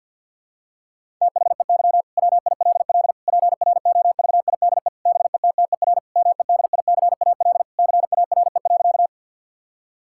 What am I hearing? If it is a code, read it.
Morse code, heard as a keyed tone, about 51 wpm, 690 Hz. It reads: THEY WILL PROVIDE BETTER MEDICAL CARE.
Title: Sentences from Top 1000 Words - Review Part 5 - 50wpm